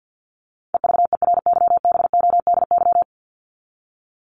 Morse code, heard as a keyed tone, about 50 wpm, 700 Hz. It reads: EVERYBODY
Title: Top 1251-1500 Words 50wpm